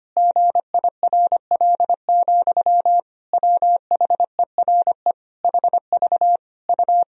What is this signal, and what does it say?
Morse code, heard as a keyed tone, about 25 wpm, 700 Hz. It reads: GIRL, WHERE HV U
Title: Sentences from Top 400 Words and Abbreviations 25wpm